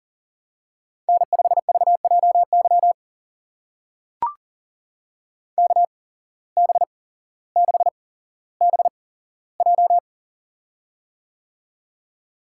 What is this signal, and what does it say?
Morse code, heard as a keyed tone, about 42 wpm, spaced as for 17 wpm, 700 Hz. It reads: N5VJY KB6BJ
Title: USA Call Signs - 5 Characters - 8x Character Spacing - 40wpm